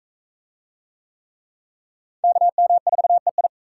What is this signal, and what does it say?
Morse code, heard as a keyed tone, about 42 wpm, 700 Hz. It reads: KM4EI